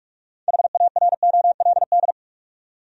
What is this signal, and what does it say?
Morse code, heard as a keyed tone, about 41 wpm, 700 Hz. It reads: HAROLD
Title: USA Common Names - 45wpm